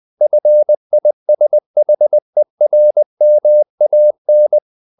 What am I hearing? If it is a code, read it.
Morse code, heard as a keyed tone, about 20 wpm, 600 Hz. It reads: FISHERMAN
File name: the old man and the sea part 2 20wpm